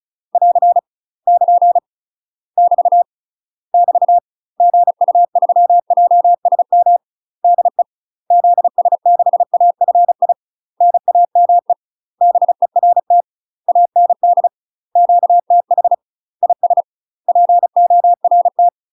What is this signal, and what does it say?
Morse code, heard as a keyed tone, about 34 wpm, 700 Hz. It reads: <AR> <KN> <BT> <BT> GU3JSM DE ZS6AFI NAME BERT AND QTH IS PORT
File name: UK MASTER QSOs TRAINING / 35WPM / UK MASTER QSOs 0019